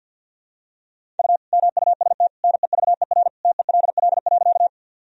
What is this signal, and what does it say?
Morse code, heard as a keyed tone, about 48 wpm, 700 Hz. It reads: U MUST NEVER TELL.
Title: Sentences from Top 300 Words and Abbreviations 50wpm